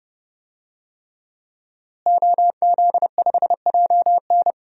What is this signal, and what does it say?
Morse code, heard as a keyed tone, about 32 wpm, 700 Hz. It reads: OZ5JN
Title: DXCC-100 Call Signs - 5 Characters - Repeated 3x - 30wpm